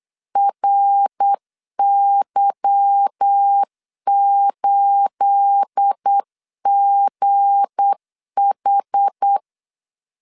Morse code, heard as a keyed tone, about 8 wpm, 800 Hz. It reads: RY8GH
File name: LEZIONI / 9 28 8wpm 3512